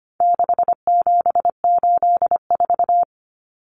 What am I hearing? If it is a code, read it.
Morse code, heard as a keyed tone, about 25 wpm, 700 Hz. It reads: 6784
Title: Common Cut Numbers - 4-Digit 25wpm